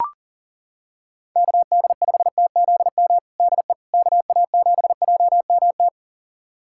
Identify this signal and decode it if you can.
Morse code, heard as a keyed tone, about 37 wpm, 700 Hz. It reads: KD5TZM DE KA7JMT
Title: USA Callsign DE Callsign 40wpm